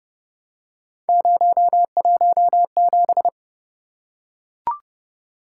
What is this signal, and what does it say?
Morse code, heard as a keyed tone, about 26 wpm, 700 Hz. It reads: 017